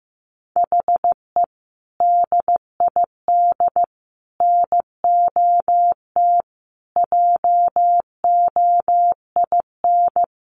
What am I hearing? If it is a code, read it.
Morse code, heard as a keyed tone, about 15 wpm, 700 Hz. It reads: HE DID NOT JOIN
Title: Sentences from Top 1400 Words 15wpm